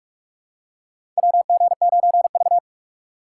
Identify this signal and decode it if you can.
Morse code, heard as a keyed tone, about 44 wpm, 700 Hz. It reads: WG9V
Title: USA Call Signs - 4 Characters - 4x Character Spacing - 45wpm